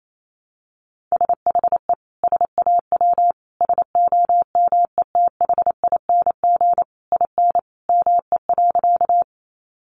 Morse code, heard as a keyed tone, about 28 wpm, 700 Hz. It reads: SHE SAW SOMETHING IN ME.
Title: Sentences from Top 600 Words 28wpm